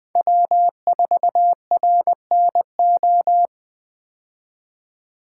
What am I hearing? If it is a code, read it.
Morse code, heard as a keyed tone, about 20 wpm, 700 Hz. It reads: W4RNO
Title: 2023 POTA Hunter Call Signs - 20wpm